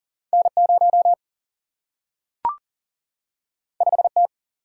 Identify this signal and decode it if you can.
Morse code, heard as a keyed tone, about 40 wpm, 700 Hz. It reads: N 0 5 T
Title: Single-Letter Single-Number 40wpm